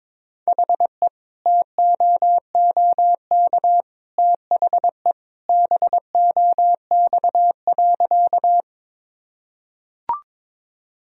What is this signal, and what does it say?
Morse code, heard as a keyed tone, about 21 wpm, 700 Hz. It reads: HE TOOK THE BOX.
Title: Sentences from Top 1000 Words - Review Part 2 - 22wpm